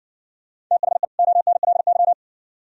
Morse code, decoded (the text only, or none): THE ONLY